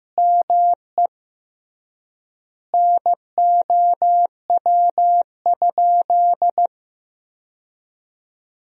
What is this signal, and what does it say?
Morse code, heard as a keyed tone, about 15 wpm, 700 Hz. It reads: ME NOW?
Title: 3x Word-Spacing - Sentences from Top 400 Words - 15wpm